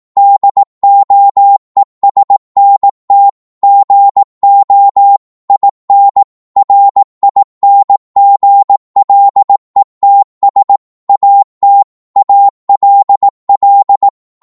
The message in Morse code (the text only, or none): DOESNT GO IN RINGLETS AT ALL